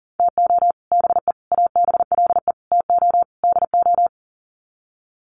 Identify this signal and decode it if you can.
Morse code, heard as a keyed tone, about 40 wpm, 700 Hz. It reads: TO BE ABLE TO DO
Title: Sets of 5 Words 40wpm